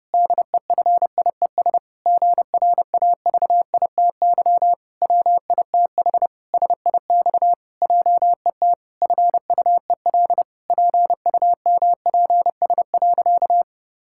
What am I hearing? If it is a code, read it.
Morse code, heard as a keyed tone, about 32 wpm, 700 Hz. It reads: DEFIES GRAVITY WITH SIX JET FUEL PUMPS.
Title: Pangram Challenge v2 - 30wpm